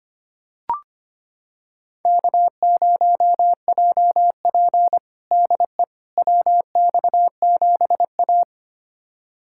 Morse code, heard as a keyed tone, about 25 wpm, 700 Hz. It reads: K0JP DE WX7A
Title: USA Callsign DE Callsign - 4 Characters - 25wpm